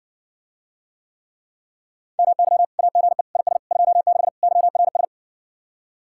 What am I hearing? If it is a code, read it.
Morse code, heard as a keyed tone, about 62 wpm, 700 Hz. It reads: MY AGE IS 26 YRS